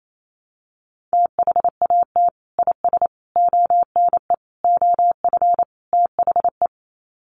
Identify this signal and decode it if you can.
Morse code, heard as a keyed tone, about 28 wpm, 700 Hz. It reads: THAT IS ONE OF THE
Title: Sets of 5 Words Encore 28wpm